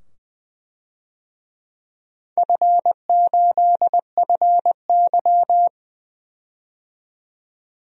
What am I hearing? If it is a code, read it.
Morse code, heard as a keyed tone, about 20 wpm, 700 Hz. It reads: F8FY
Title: DXCC-100 Call Signs - 4 Characters - 20wpm